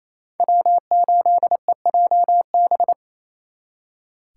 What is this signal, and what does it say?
Morse code, heard as a keyed tone, about 35 wpm, 700 Hz. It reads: W8EJB
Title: Speed Racing - USA 1x3 Call Signs 28wpm